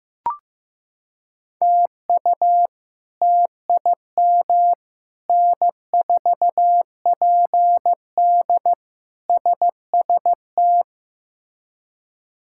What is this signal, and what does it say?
Morse code, heard as a keyed tone, about 16 wpm, 700 Hz. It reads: TU TIM N4PD SST
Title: K1USN SST QSO Practice - 15wpm